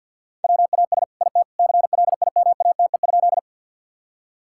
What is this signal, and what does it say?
Morse code, heard as a keyed tone, about 50 wpm, 700 Hz. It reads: WAS IT CLIMATE?
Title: Sentences from Top 1600 Words 50wpm